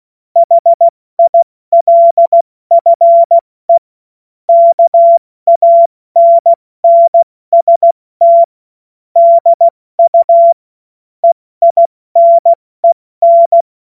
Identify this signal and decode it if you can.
Morse code, heard as a keyed tone, about 16 wpm, 680 Hz. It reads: HILFE KANNST DU EINEN
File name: Morsekurs / Witze / Witz 02. Dezember 2025 70 F80